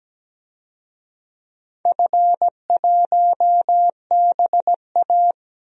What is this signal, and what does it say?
Morse code, heard as a keyed tone, about 17 wpm, 700 Hz. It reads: F1BA
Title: DXCC-100 Call Signs - 4 Characters - Repeated 2x - 17wpm